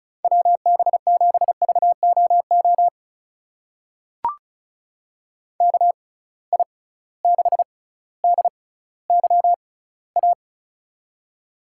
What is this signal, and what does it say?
Morse code, heard as a keyed tone, about 36 wpm, spaced as for 18 wpm, 700 Hz. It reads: WB7VOO KI6DYA
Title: USA Call Signs - 6 Characters - 6x Character Spacing - 35wpm